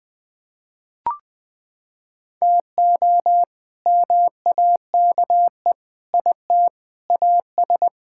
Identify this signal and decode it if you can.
Morse code, heard as a keyed tone, about 21 wpm, 700 Hz. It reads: TO MAKE IT AS
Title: Sets of 8 Words - 20wpm